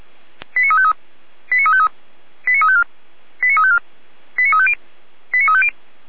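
Five-tone selective call (ZVEI), heard as quick runs of five tones, tones of about 70 ms. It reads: 89353 89353 89354 89354 89350 89350
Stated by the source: ZVEI five-tone selective call received with an SDR.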